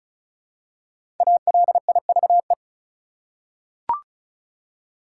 Morse code, heard as a keyed tone, about 36 wpm, 700 Hz. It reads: ALIVE
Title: Five Letter Words - 35wpm